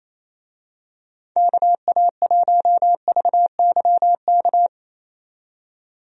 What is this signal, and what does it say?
Morse code, heard as a keyed tone, about 28 wpm, 700 Hz. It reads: KA1VYK
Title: USA Call Signs - 6 Characters - 28wpm